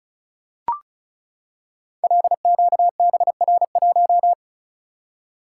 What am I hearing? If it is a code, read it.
Morse code, heard as a keyed tone, about 36 wpm, 700 Hz. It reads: LQBR1
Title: Code Groups - 5 Letter:Numbers - 35wpm